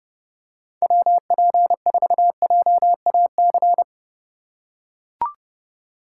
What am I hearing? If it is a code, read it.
Morse code, heard as a keyed tone, about 30 wpm, 700 Hz. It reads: WP4JAC
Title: USA Call Signs - 6 Characters - 2x Character Spacing - 30wpm